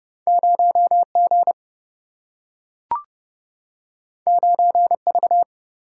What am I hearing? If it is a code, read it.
Morse code, heard as a keyed tone, about 30 wpm, 700 Hz. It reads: 0G 9V
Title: Single-Letter Single-Number 30wpm